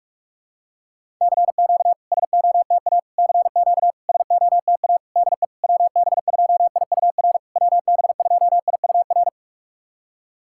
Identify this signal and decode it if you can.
Morse code, heard as a keyed tone, about 44 wpm, 690 Hz. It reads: CQ SOTA CQ SOTA DE WB2IUR WB2IUR
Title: SOTA QSO Practice 45wpm